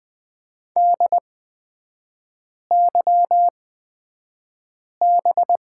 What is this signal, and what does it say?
Morse code, heard as a keyed tone, about 20 wpm, spaced as for 5 wpm, 700 Hz. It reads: DYB
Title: 229 Lesson 37e - Letter x - Three Characters 20wpm